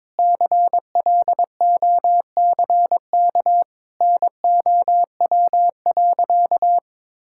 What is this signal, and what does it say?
Morse code, heard as a keyed tone, about 22 wpm, 700 Hz. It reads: CLOCK NOW.
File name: practice / 22wpm / Easy Sentences with Hints - 22wpm